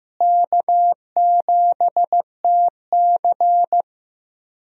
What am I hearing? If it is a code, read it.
Morse code, heard as a keyed tone, about 15 wpm, 700 Hz. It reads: K7TC
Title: USA Call Signs - 4 Characters - Repeated 2x - 15wpm